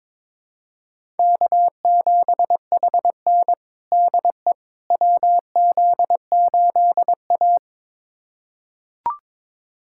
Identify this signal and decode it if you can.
Morse code, heard as a keyed tone, about 22 wpm, 700 Hz. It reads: K7HN DE WZ8A
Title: USA Callsign DE Callsign - 4 Characters - 22wpm